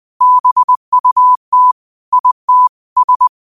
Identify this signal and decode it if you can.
Morse code, heard as a keyed tone, about 20 wpm, 1000 Hz. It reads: BUT IT S
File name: qotdfiles / 20wpm